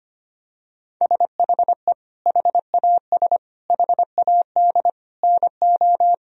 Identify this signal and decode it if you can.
Morse code, heard as a keyed tone, about 25 wpm, 700 Hz. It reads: SHE HAS HAD NO
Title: Sentences from Top 1000 Words - Review Part 4 - 25wpm